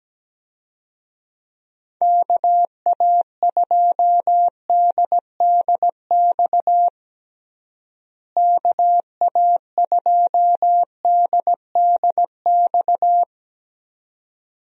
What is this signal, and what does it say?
Morse code, heard as a keyed tone, about 17 wpm, 700 Hz. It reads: KA2DDX KA2DDX
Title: USA Call Signs - 6 Characters - Repeated 2x - 17wpm